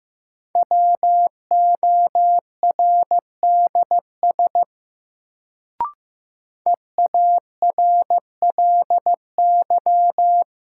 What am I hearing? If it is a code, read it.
Morse code, heard as a keyed tone, about 15 wpm, 700 Hz. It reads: WORDS EARLY
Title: Sets of 3 Words - New Word Drill - 15wpm